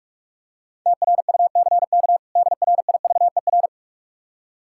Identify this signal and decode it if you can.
Morse code, heard as a keyed tone, about 43 wpm, 700 Hz. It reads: TRUCK DRIVER
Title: K1USN SST QSO Practice 45wpm